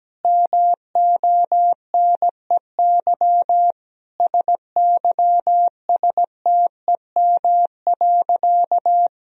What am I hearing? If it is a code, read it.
Morse code, heard as a keyed tone, about 17 wpm, 700 Hz. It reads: MONEY SYSTEM.
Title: Sentences from Top 1000 Words - Review Part 4 - 17wpm